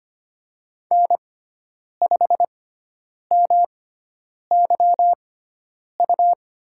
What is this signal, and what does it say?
Morse code, heard as a keyed tone, about 25 wpm, spaced as for 9 wpm, 700 Hz. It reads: N5MYU
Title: USA Call Signs - 5 Characters - 6x Character Spacing - 25wpm